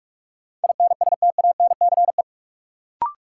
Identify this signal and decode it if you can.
Morse code, heard as a keyed tone, about 41 wpm, 700 Hz. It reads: INSTANCE
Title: Sentences from Top 800 Words - New Word Drill - 45wpm